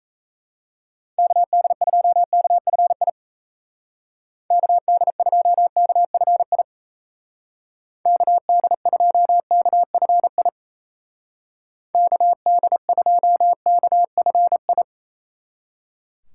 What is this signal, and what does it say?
Morse code, heard as a keyed tone, about 37 wpm, 700 Hz. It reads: KD2KFI KD2KFI KD2KFI KD2KFI